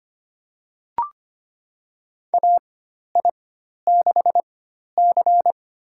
Morse code, heard as a keyed tone, about 25 wpm, spaced as for 12 wpm, 700 Hz. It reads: AI6C
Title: USA Call Signs - 4 Characters - 4x Character Spacing - 25wpm